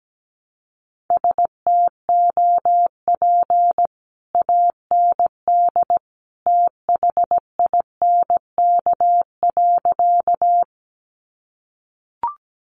Morse code, heard as a keyed tone, about 17 wpm, 700 Hz. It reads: STOP AND THINK.
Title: Sentences from Top 300 Words 17wpm